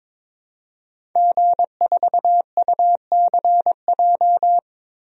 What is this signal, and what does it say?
Morse code, heard as a keyed tone, about 22 wpm, 700 Hz. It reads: G4UCJ
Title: DXCC-100 Call Signs - 5 Characters - Repeated 2x - 22wpm